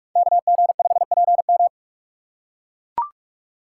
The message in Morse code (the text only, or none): KG5PM